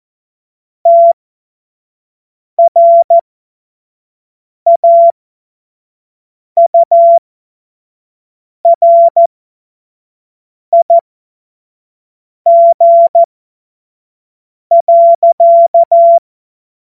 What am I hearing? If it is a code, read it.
Morse code, heard as a keyed tone, about 14 wpm, spaced as for 5 wpm, 680 Hz. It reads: TRAURIG.
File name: Morsekurs / Witze / Witz 11. Mai 2021, 25 F70